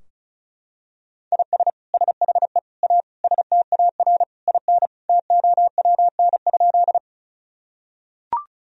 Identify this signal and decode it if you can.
Morse code, heard as a keyed tone, about 34 wpm, 700 Hz. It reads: IS SHE A STAR IN TOWN?
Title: Sentences from Top 1000 Words - Review Part 2 - 35wpm